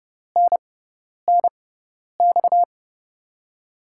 Morse code, heard as a keyed tone, about 31 wpm, spaced as for 10 wpm, 700 Hz. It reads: NNX
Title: USA Call Signs - 6 Characters - 6x Character Spacing - 30wpm